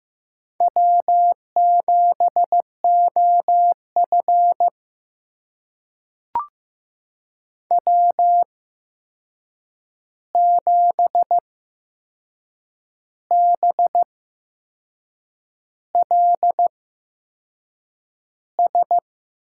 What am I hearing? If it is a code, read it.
Morse code, heard as a keyed tone, about 15 wpm, spaced as for 6 wpm, 700 Hz. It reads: W7OF W7BLS
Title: USA Call Signs - 8x Character Spacing - 15wpm